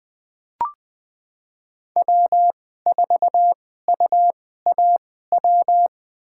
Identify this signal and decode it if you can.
Morse code, heard as a keyed tone, about 20 wpm, spaced as for 14 wpm, 700 Hz. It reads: W4UAW
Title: USA Call Signs - 5 Characters - 2x Character Spacing - 20wpm